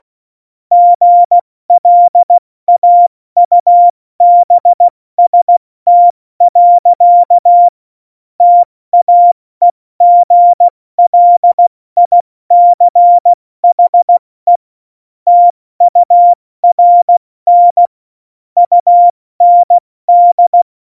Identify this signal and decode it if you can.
Morse code, heard as a keyed tone, about 16 wpm, 700 Hz. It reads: GLAUBST. TAEGLICHE TURN UND
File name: Morsekurs / Witze / Witz 19. Dezember 2017, 70 F80